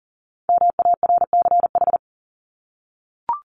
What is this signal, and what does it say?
Morse code, heard as a keyed tone, about 40 wpm, 700 Hz. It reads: MARCH
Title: Top 1501-1750 Words 40wpm